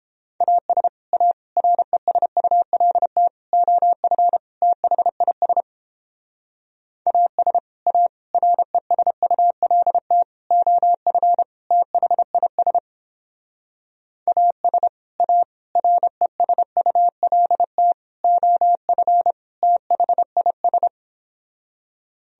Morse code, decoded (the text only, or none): AS A RESULT OF THIS AS A RESULT OF THIS AS A RESULT OF THIS